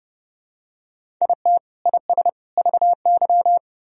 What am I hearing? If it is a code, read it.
Morse code, heard as a keyed tone, about 32 wpm, 700 Hz. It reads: IT IS VY
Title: Sentences from Top 500 Words and Abbreviations 30wpm